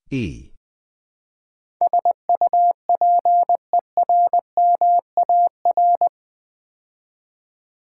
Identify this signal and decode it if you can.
Morse code, heard as a keyed tone, about 20 wpm, 700 Hz. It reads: SUPERMAR
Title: Build A Word - 4 Letters At A Time - 11-Letter Words - 20wpm